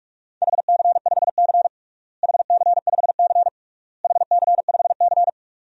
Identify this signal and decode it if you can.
Morse code, heard as a keyed tone, about 46 wpm, 700 Hz. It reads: HC5C HC5C HC5C
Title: DXCC-100 Call Signs - 4 Characters - Repeated 3x - 45wpm